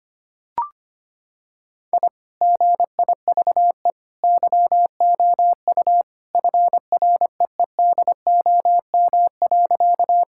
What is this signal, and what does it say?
Morse code, heard as a keyed tone, about 25 wpm, 700 Hz. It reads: I GIVE YOU FREEDOM.
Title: Sentences from Top 1000 Words - Review Part 5 - 25wpm